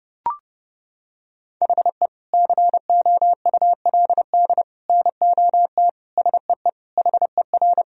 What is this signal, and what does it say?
Morse code, heard as a keyed tone, about 30 wpm, 700 Hz. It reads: HE COULD NOT SEE HER
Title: Sentences from Top 1000 Words - Review Part 2 - 30wpm